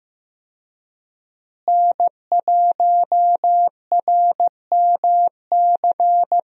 Morse code, heard as a keyed tone, about 15 wpm, 700 Hz. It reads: N1RMC
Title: USA Call Signs - 5 Characters - Repeated 3x - 15wpm